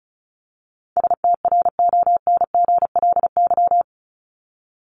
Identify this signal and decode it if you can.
Morse code, heard as a keyed tone, about 34 wpm, 700 Hz. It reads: STRONGLY
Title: LY Suffix - Top 100 Words with Base Words 35wpm